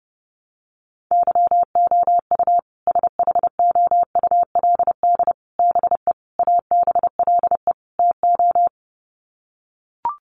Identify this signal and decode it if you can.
Morse code, heard as a keyed tone, about 30 wpm, 700 Hz. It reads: YOU SHOULD BE ABLE TO
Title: Sets of 5 Words Encore 30wpm